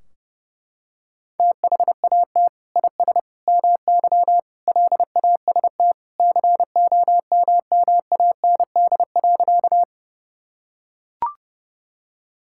Morse code, heard as a keyed tone, about 30 wpm, 700 Hz. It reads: THAT IS MY LAST COMMAND.
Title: Sentences from Top 1600 Words 30wpm